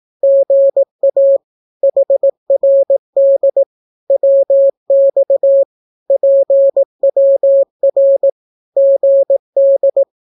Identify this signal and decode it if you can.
Morse code, heard as a keyed tone, about 18 wpm, 550 Hz. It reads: GA HRD WX PWR GD
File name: naukapodstaw / skroty / skroty